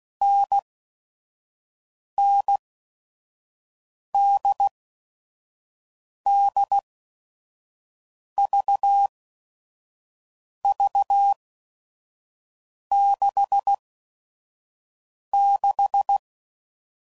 Morse code, heard as a keyed tone, about 16 wpm, spaced as for 5 wpm, 800 Hz. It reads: NN DD VV 66
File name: Mixed / Short Runs / NDB NDB6